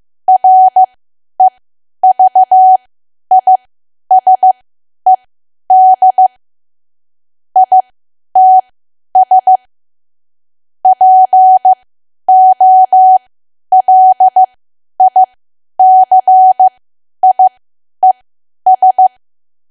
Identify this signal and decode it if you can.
Morse code, heard as a keyed tone, about 15 wpm, spaced as for 10 wpm, 750 Hz. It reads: REVISED ITS POLICIES
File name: cwa / W1AW-Level2 / 10WPM